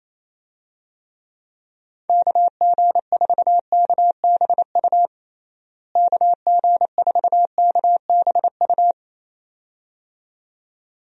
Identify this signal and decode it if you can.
Morse code, heard as a keyed tone, about 28 wpm, 700 Hz. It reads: KG4KBU KG4KBU
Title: USA Call Signs - 6 Characters - Repeated 2x - 28wpm